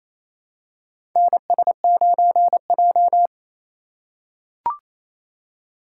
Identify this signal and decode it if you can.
Morse code, heard as a keyed tone, about 28 wpm, 700 Hz. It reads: NS9J
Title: USA Call Signs - 4 Characters - 2x Character Spacing - 28wpm